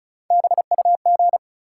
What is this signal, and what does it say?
Morse code, heard as a keyed tone, about 36 wpm, 700 Hz. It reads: BUG
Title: Three Letter Words - 35wpm